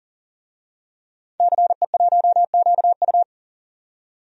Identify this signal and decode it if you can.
Morse code, heard as a keyed tone, about 40 wpm, 700 Hz. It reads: CE1QU
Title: DXCC-100 Call Signs - 5 Characters - 40wpm